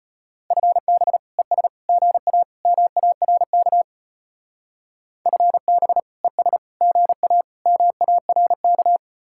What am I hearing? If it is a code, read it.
Morse code, heard as a keyed tone, about 33 wpm, 700 Hz. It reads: FB ES GA MARK FB ES GA MARK